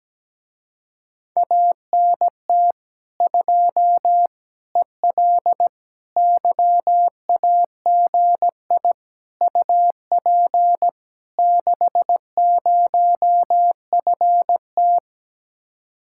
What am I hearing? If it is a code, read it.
Morse code, heard as a keyed tone, about 17 wpm, 700 Hz. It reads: ANT 2 EL YAGI UP 60FT
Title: Speed-Racing Common QSO Phrases - 17wpm